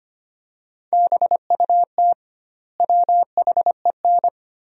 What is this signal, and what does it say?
Morse code, heard as a keyed tone, about 25 wpm, 700 Hz. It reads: BUT WHEN
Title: 2x Word-Spacing - Sentences from Top 100 Words - 25wpm